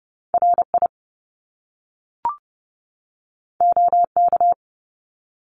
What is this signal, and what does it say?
Morse code, heard as a keyed tone, about 30 wpm, 700 Hz. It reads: RI OK
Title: Top Two Letter Combinations in Words 30wpm